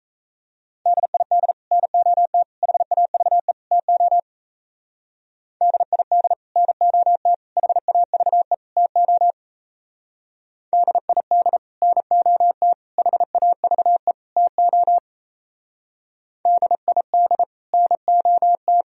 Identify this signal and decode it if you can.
Morse code, heard as a keyed tone, about 35 wpm, 700 Hz. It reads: DID NOT HAVE TO DID NOT HAVE TO DID NOT HAVE TO DID NOT
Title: Speed-Racing - Sets of 4 Words 28wpm